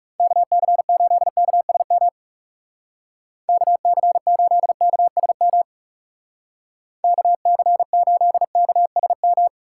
Morse code, heard as a keyed tone, about 36 wpm, 700 Hz. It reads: KC8KSM KC8KSM KC8KSM
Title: Speed Racing - USA Call Signs 30wpm